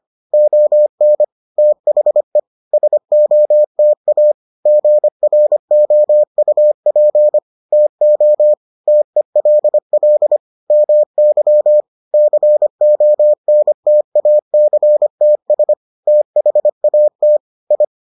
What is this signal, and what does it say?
Morse code, heard as a keyed tone, about 25 wpm, 600 Hz. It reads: ON THE SOTA GROUP TO TELL MY CONTACTS THAT I